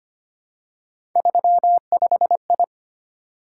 Morse code, heard as a keyed tone, about 25 wpm, 700 Hz. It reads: 35I
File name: Three Letter-Number Combinations - 25wpm